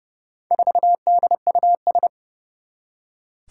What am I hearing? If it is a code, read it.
Morse code, heard as a keyed tone, about 30 wpm, 700 Hz. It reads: 4DUS
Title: Four Challenging Letter-Number Combinations 30wpm